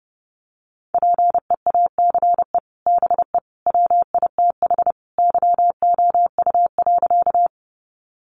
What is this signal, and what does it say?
Morse code, heard as a keyed tone, about 30 wpm, 700 Hz. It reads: PEACE BE WITH YOU.
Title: Sentences from Top 1500 Words 30wpm